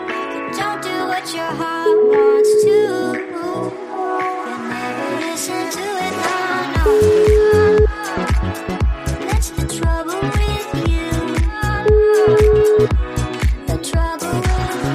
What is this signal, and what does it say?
Ringback tone (Europe, 425 Hz), with noise.